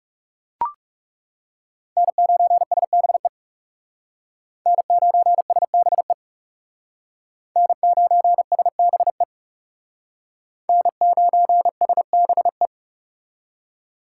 Morse code, heard as a keyed tone, about 40 wpm, 700 Hz. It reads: N9SBE N9SBE N9SBE N9SBE